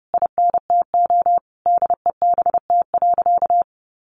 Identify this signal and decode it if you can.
Morse code, heard as a keyed tone, about 30 wpm, 700 Hz. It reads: INTO DEBT.
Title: Sentences from Top 1300 Words 30wpm